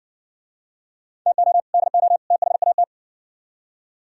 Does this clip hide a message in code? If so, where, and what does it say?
Yes: Morse code from 1.3 to 2.8 s, heard as a keyed tone, about 56 wpm, 700 Hz. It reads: TO DO THAT